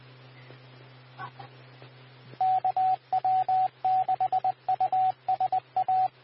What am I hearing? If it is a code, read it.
Morse code, heard as a keyed tone, about 20 wpm, 750 Hz, with radio noise. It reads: KW6USA
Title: W6NUT 147.450 MHz Repeater